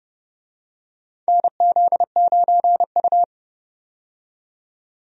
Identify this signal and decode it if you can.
Morse code, heard as a keyed tone, about 30 wpm, 700 Hz. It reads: NZ9U
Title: USA Call Signs - 4 Characters - 30wpm